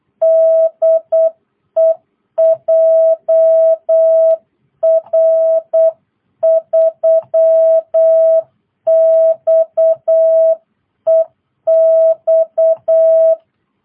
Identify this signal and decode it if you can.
Morse code, heard as a keyed tone, about 8 wpm, 660 Hz. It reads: DE JR3XEX